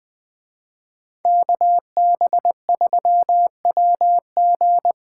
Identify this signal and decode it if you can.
Morse code, heard as a keyed tone, about 20 wpm, 700 Hz. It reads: KB3WG
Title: Speed Racing - USA 2x2 Call Signs 17wpm